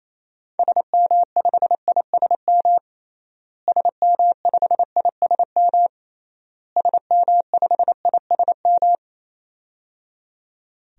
Morse code, heard as a keyed tone, about 28 wpm, 700 Hz. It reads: SM5ISM SM5ISM SM5ISM
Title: DXCC-100 Call Signs - 6 Characters - Repeated 3x - 28wpm